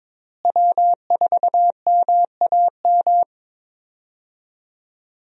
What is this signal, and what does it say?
Morse code, heard as a keyed tone, about 22 wpm, 700 Hz. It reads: W4MAM
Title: USA Call Signs - 5 Characters - Repeated 2x - 22wpm